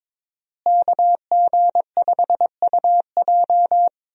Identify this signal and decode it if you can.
Morse code, heard as a keyed tone, about 22 wpm, 700 Hz. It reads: KG5UJ
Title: USA Call Signs - 5 Characters - Repeated 3x - 22wpm